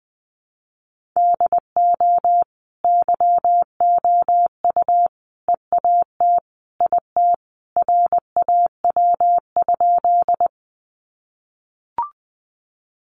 Morse code, heard as a keyed tone, about 20 wpm, 700 Hz. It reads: DO YOU EAT IT RAW?
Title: Sentences from Top 1500 Words 20wpm